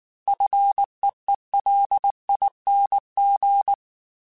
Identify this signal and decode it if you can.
Morse code, heard as a keyed tone, about 19 wpm, 800 Hz. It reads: FEELING
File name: F and L / F and L mp3 / F and L Prefixes 3x2 Twice @18